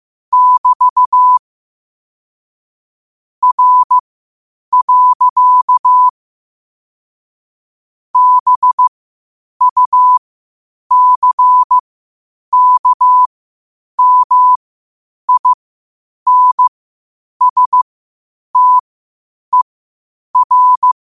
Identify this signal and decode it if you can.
Morse code, heard as a keyed tone, about 15 wpm, spaced as for 8 wpm, 1000 Hz. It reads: <BT> R. BUCKMINSTER
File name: 8wpm